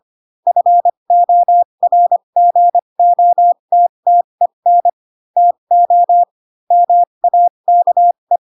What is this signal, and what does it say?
Morse code, heard as a keyed tone, about 25 wpm, spaced as for 22 wpm, 700 Hz. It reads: FORGOTTEN TO MAKE
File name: Pinocchio / Pinocchio ch3 18 25 ik1hge